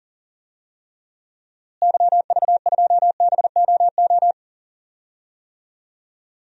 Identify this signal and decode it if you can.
Morse code, heard as a keyed tone, about 40 wpm, 700 Hz. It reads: YV2BOO
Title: DXCC-100 Call Signs - 6 Characters - Repeated 2x - 40wpm